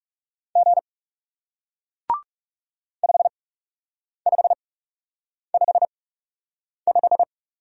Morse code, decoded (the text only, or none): G 5555